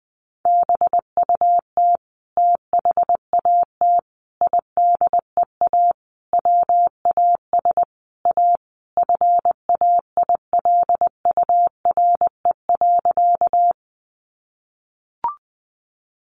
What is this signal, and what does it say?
Morse code, heard as a keyed tone, about 20 wpm, 700 Hz. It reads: BUT THAT IDEA WAS A FAILURE.
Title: Sentences from Top 900 Words 20wpm